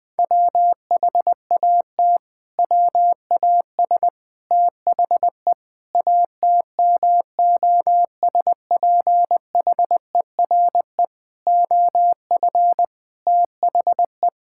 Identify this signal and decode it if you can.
Morse code, heard as a keyed tone, about 19 wpm, 700 Hz. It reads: WHAT WAS THE ATMOSPHERE OF THE
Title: Sentences from Top 1600 Words 20wpm